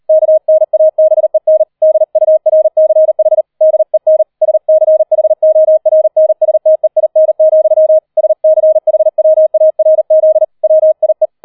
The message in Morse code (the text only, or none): KNABEN DURCH DEN SCHORNSTEIN, SCHWARZ WIE